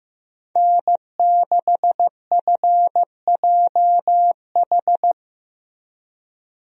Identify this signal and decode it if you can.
Morse code, heard as a keyed tone, about 15 wpm, 700 Hz. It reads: N6FJH
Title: USA Call Signs - 5 Characters - Repeated 3x - 15wpm